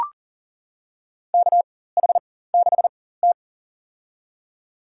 Morse code, heard as a keyed tone, about 39 wpm, spaced as for 18 wpm, 700 Hz. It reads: KH6T
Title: USA Call Signs - 4 Characters - 4x Character Spacing - 40wpm